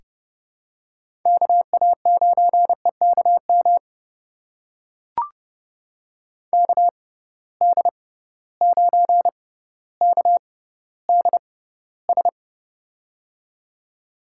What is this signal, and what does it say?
Morse code, heard as a keyed tone, about 30 wpm, spaced as for 15 wpm, 700 Hz. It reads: KA9EKM KD9KDS
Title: USA Call Signs - 6 Characters - 6x Character Spacing - 30wpm